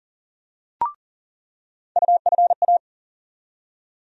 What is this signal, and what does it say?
Morse code, heard as a keyed tone, about 39 wpm, 700 Hz. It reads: UFA
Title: USA Call Sign Postfixes - 3 Letters 40wpm